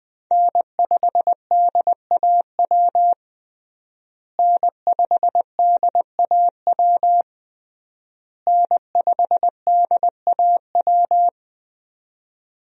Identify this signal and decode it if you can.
Morse code, heard as a keyed tone, about 21 wpm, 700 Hz. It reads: N5DAW N5DAW N5DAW
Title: USA Call Signs - 5 Characters - Repeated 3x - 20wpm